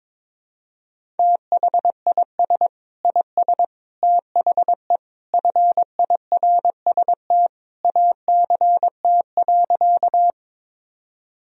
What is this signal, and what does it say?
Morse code, heard as a keyed tone, about 21 wpm, 700 Hz. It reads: THIS IS THE FIRST ACT.
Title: Sentences from Top 1000 Words - Review Part 1 - 22wpm